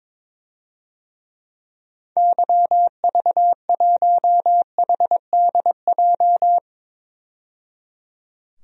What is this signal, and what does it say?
Morse code, heard as a keyed tone, about 22 wpm, 700 Hz. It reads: YV1HDJ
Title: DXCC-100 Call Signs - 6 Characters - 22wpm